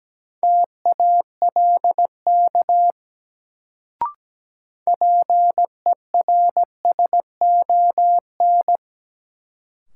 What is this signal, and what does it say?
Morse code, heard as a keyed tone, about 17 wpm, 700 Hz. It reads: TALK PERSON
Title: Sentences from Top 400 Words - New Word Drill - 17wpm